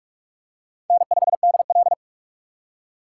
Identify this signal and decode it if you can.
Morse code, heard as a keyed tone, about 46 wpm, 700 Hz. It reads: N5DL
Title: USA Call Signs - 3x Character Spacing - 45wpm